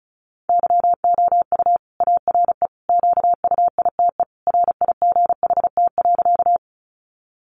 Morse code, heard as a keyed tone, about 34 wpm, 700 Hz. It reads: YOU ARE QUITE RIGHT.
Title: Sentences from Top 700 Words 35wpm